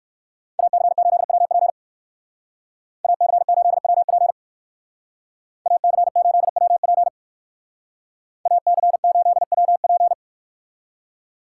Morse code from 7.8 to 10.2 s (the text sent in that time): AC8WP